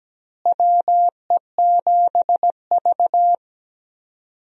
Morse code, heard as a keyed tone, about 17 wpm, 700 Hz. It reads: WE7V
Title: USA Call Signs - 4 Characters - Repeated 2x - 17wpm